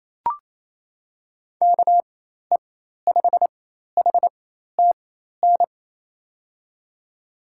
Morse code, heard as a keyed tone, about 28 wpm, spaced as for 13 wpm, 700 Hz. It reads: KE5HTN
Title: USA Call Signs - 4x Character Spacing - 28wpm